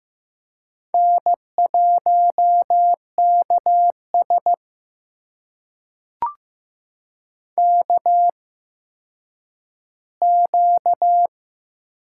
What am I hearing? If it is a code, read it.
Morse code, heard as a keyed tone, about 15 wpm, spaced as for 9 wpm, 700 Hz. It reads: N1KS KQ
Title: USA Call Signs - 4 Characters - 8x Character Spacing - 15wpm